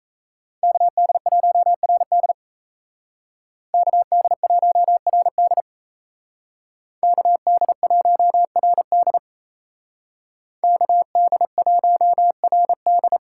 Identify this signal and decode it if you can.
Morse code, heard as a keyed tone, about 37 wpm, 700 Hz. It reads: KD1RD KD1RD KD1RD KD1RD